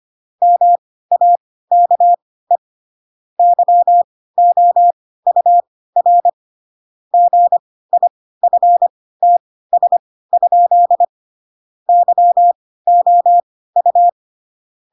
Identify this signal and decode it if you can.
Morse code, heard as a keyed tone, about 25 wpm, spaced as for 16 wpm, 700 Hz. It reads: MAKE YOUR GIFTS? YOU
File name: Pinocchio / Pinocchio ch2 10 25 ik1hge